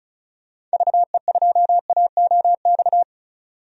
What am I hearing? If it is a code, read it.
Morse code, heard as a keyed tone, about 36 wpm, 690 Hz. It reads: VE2AOX
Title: DXCC-100 Call Signs - 6 Characters - 35wpm